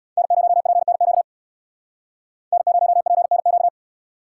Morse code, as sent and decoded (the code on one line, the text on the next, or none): -. ----- .--. -. --.- / -. ----- .--. -. --.-
N0PNQ N0PNQ